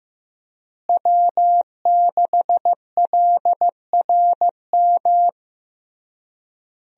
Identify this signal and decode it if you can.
Morse code, heard as a keyed tone, about 15 wpm, 700 Hz. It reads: W6LRM
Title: USA Call Signs - 5 Characters - Repeated 3x - 15wpm